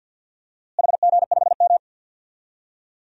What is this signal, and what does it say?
Morse code, heard as a keyed tone, about 52 wpm, 700 Hz. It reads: HG5M
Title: DXCC-100 Call Signs - 4 Characters - Repeated 2x - 50wpm